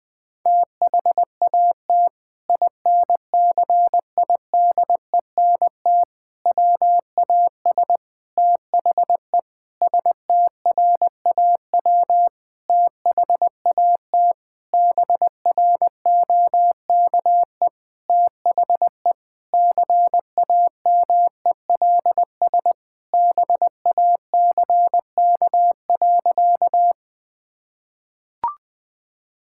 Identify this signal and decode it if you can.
Morse code, heard as a keyed tone, about 20 wpm, 700 Hz. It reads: THAT INCIDENT WAS THE STRAW THAT BROKE THE CAMELS BACK.
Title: Sentences with 8-Word Phrases - Spoken Beginning - 20wpm